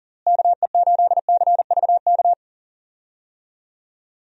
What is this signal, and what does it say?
Morse code, heard as a keyed tone, about 40 wpm, 700 Hz. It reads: KE8CVK
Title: USA Call Signs - 6 Characters - Repeated 2x - 40wpm